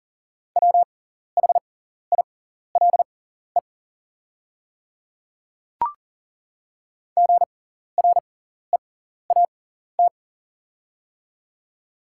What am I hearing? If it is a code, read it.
Morse code, heard as a keyed tone, about 40 wpm, spaced as for 14 wpm, 690 Hz. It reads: WHILE GREAT
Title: 6x Character Spacing - Top 300 Words - 40wpm